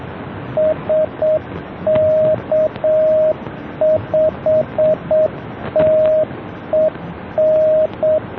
Morse code, heard as a keyed tone, about 7 wpm, 620 Hz, with radio noise. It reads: SK5TEN